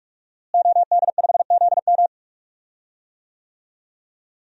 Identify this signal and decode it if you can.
Morse code, heard as a keyed tone, about 42 wpm, 700 Hz. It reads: OD5ZM
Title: DXCC-100 Call Signs - 5 Characters - Repeated 3x - 45wpm